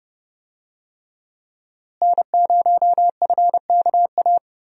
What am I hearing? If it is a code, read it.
Morse code, heard as a keyed tone, about 32 wpm, 700 Hz. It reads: N0FKA
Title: USA Call Signs - 5 Characters - Repeated 3x - 30wpm